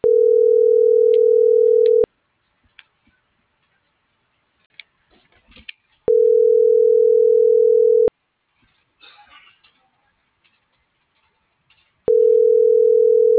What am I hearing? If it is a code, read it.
Ringback tone (North America), 440 + 480 Hz.
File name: CENTRAL BLN / out ximena.diaz